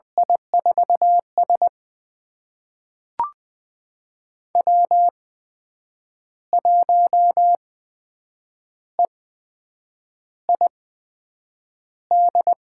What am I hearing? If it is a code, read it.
Morse code, heard as a keyed tone, about 20 wpm, spaced as for 7 wpm, 700 Hz. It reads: I4S W1EID
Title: USA Call Signs - 8x Character Spacing - 20wpm